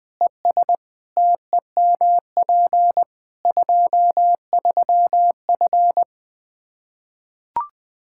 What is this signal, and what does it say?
Morse code, heard as a keyed tone, about 20 wpm, 700 Hz. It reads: ES TEMP 23F
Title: K1USN SST QSO Practice 20wpm